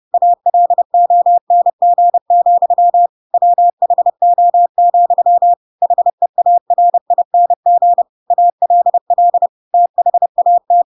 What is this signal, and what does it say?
Morse code, heard as a keyed tone, about 30 wpm, 700 Hz. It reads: ALONG, WHO, HEARING ALL THAT
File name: Pinocchio / Pinocchio ch3 30 30 ik1hge